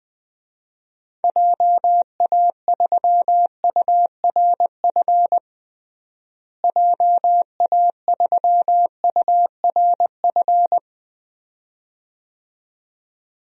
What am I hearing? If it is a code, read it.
Morse code, heard as a keyed tone, about 20 wpm, 700 Hz. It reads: JA3URF JA3URF
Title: DXCC-100 Call Signs - 6 Characters - Repeated 2x - 20wpm